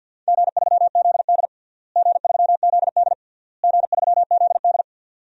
Morse code, heard as a keyed tone, about 49 wpm, 700 Hz. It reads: G3ZD G3ZD G3ZD
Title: DXCC-100 Call Signs - 4 Characters - Repeated 3x - 50wpm